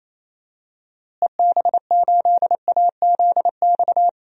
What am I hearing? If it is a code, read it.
Morse code, heard as a keyed tone, about 28 wpm, 700 Hz. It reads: EB8AZX
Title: DXCC-100 Call Signs - 6 Characters - Repeated 2x - 28wpm